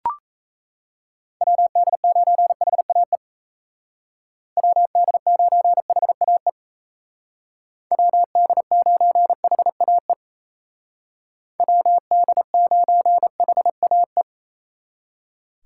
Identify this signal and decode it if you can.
Morse code, heard as a keyed tone, about 33 wpm, 700 Hz. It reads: WD9HAE WD9HAE WD9HAE WD9HAE
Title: Speed Racing - USA 2x3 Call Signs 28wpm